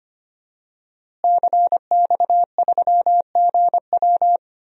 Morse code, heard as a keyed tone, about 25 wpm, 700 Hz. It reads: CX3GW
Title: DXCC-100 Call Signs - 5 Characters - Repeated 2x - 25wpm